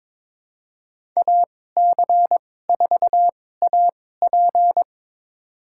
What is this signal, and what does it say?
Morse code, heard as a keyed tone, about 21 wpm, spaced as for 16 wpm, 700 Hz. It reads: AC4AP
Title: USA Call Signs - 5 Characters - 2x Character Spacing - 22wpm